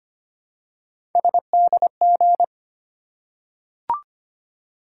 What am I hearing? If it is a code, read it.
Morse code, heard as a keyed tone, about 25 wpm, 700 Hz. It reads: SDG
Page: morse-code-ninja-practice-25wpm